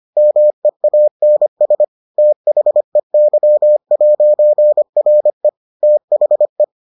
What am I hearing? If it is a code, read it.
Morse code, heard as a keyed tone, about 25 wpm, 600 Hz. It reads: MEANS THEY'RE THE